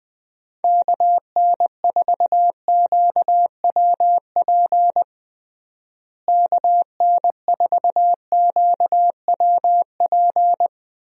Morse code, heard as a keyed tone, about 20 wpm, 700 Hz. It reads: KN4QWP KN4QWP
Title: USA Call Signs - 6 Characters - Repeated 3x - 20wpm